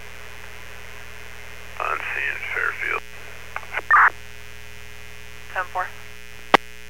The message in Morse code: none

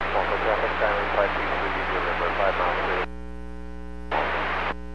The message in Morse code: none

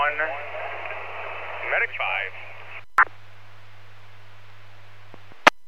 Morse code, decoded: none

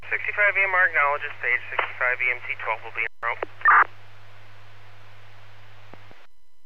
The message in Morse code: none

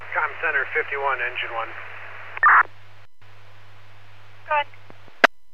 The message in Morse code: none

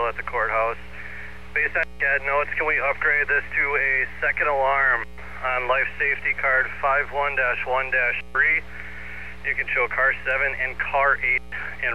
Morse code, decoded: none

